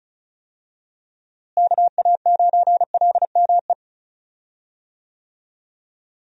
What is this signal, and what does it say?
Morse code, heard as a keyed tone, about 34 wpm, 700 Hz. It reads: KA9LME
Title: USA Call Signs - 6 Characters - Repeated 3x - 35wpm